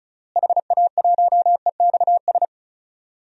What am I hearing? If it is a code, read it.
Morse code, heard as a keyed tone, about 36 wpm, 700 Hz. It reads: HA1EXS
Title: DXCC-100 Call Signs - 6 Characters - Repeated 3x - 35wpm